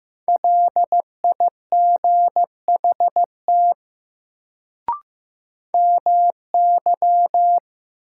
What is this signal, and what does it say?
Morse code, heard as a keyed tone, about 15 wpm, 700 Hz. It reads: LIGHT MY
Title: Sentences from Top 200 Words - New Word Drill - 15wpm